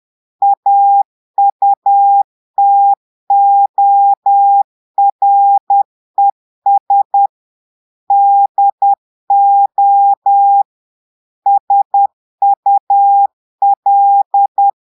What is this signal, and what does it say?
Morse code, heard as a keyed tone, about 10 wpm, 800 Hz. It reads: AUTORES DO SUL
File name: Portugais - texte en clair 1 - vitesse - 10 mots minute